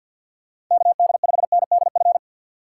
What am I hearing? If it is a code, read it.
Morse code, heard as a keyed tone, about 51 wpm, 700 Hz. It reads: KD5NDF
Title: USA Call Signs - 6 Characters - 3x Character Spacing - 50wpm